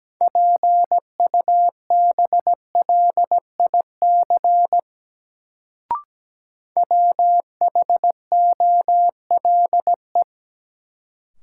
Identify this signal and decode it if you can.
Morse code, heard as a keyed tone, about 17 wpm, 700 Hz. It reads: PUBLIC WHOLE
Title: Sets of 2 Words - New Word Drill - 17wpm